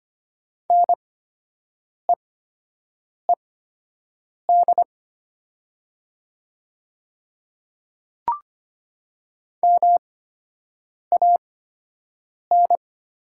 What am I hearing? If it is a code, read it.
Morse code, heard as a keyed tone, about 25 wpm, spaced as for 7 wpm, 700 Hz. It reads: NEED MAN
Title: 8x Character Spacing - Top 500 Words - 25wpm